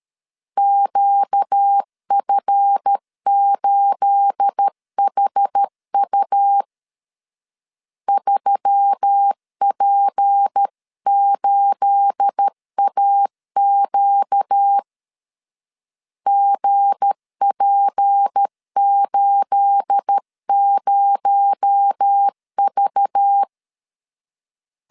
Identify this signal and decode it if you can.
Morse code, heard as a keyed tone, about 13 wpm, 800 Hz. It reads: QF8HU 3P8AQ GP80V
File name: LEZIONI / 9 28 12wpm 2351